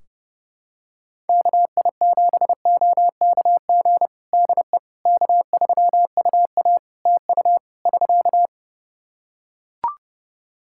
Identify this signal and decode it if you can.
Morse code, heard as a keyed tone, about 32 wpm, 700 Hz. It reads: KI7OKG DE K3UA TU <SK>
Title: Practice QSO - Exchange 4 - Ending - 30wpm